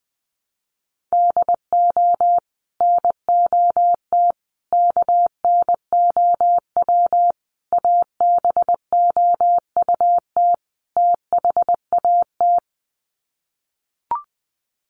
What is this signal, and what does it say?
Morse code, heard as a keyed tone, about 20 wpm, 700 Hz. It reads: DO NOT KNOW ABOUT THAT
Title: Sets of 5 Words Encore2 20wpm